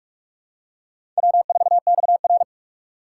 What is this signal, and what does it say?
Morse code, heard as a keyed tone, about 46 wpm, 700 Hz. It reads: W4XR
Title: USA Call Signs - 4 Characters - 2x Character Spacing - 45wpm